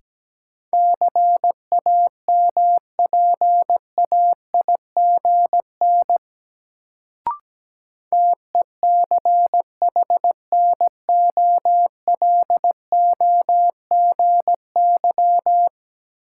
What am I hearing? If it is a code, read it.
Morse code, heard as a keyed tone, about 17 wpm, 700 Hz. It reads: CAMPAIGN TECHNOLOGY
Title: Sentences from Top 1200 Words - New Word Drill Alt - 17wpm